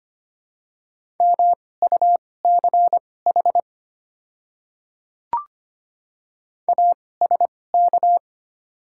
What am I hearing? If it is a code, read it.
Morse code, heard as a keyed tone, about 25 wpm, spaced as for 18 wpm, 690 Hz. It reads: MUCH ASK
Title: 2x Character Spacing - Top 200 Words - 25wpm